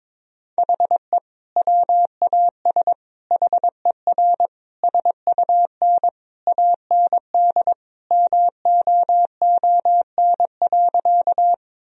Morse code, heard as a keyed tone, about 22 wpm, 690 Hz. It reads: HE WAS HER SUN AND MOON.
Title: Sentences from Top 1000 Words - Review Part 3 - 22wpm